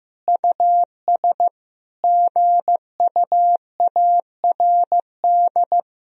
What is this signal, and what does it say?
Morse code, heard as a keyed tone, about 16 wpm, 700 Hz. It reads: US GUARD
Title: Sentences from Top 1000 Words - Review Part 4 - 15wpm